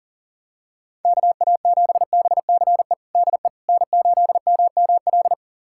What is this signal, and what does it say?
Morse code, heard as a keyed tone, about 40 wpm, 700 Hz. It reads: KA7BCE DE N8MML
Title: USA Callsign DE Callsign 40wpm